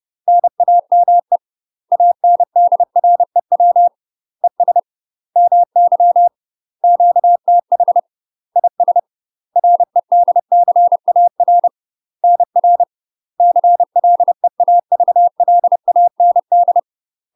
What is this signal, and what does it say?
Morse code, heard as a keyed tone, about 30 wpm, 700 Hz. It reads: NAME ANDREW ES MY QTH IS REDCAR NR CLEAVLAND
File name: UK MASTER QSOs TRAINING / 30WPM / UK MASTER QSOs 0013